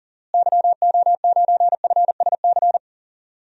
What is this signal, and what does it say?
Morse code, heard as a keyed tone, about 40 wpm, 700 Hz. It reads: YO9FSC
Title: DXCC-100 Call Signs - 6 Characters - Repeated 2x - 40wpm